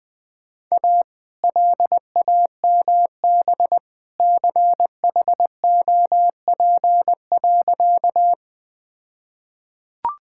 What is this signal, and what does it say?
Morse code, heard as a keyed tone, about 20 wpm, 700 Hz. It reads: A LAMB CHOP.
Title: All-Age Appropriate Jokes - 20wpm